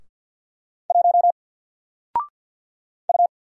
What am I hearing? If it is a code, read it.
Morse code, heard as a keyed tone, about 48 wpm, 700 Hz. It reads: 1 U
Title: Single Character (Letter:Number:Punctuation:Prosign) 50wpm